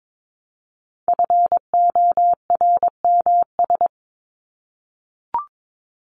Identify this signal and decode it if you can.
Morse code, heard as a keyed tone, about 22 wpm, 700 Hz. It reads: FORMS
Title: ES:S Suffix - Top 100 Words with Base Words 22wpm